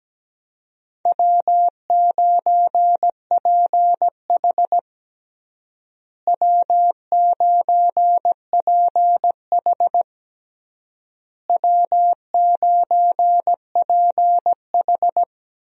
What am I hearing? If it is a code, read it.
Morse code, heard as a keyed tone, about 17 wpm, 700 Hz. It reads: W9PH W9PH W9PH
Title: USA Call Signs - 4 Characters - Repeated 3x - 17wpm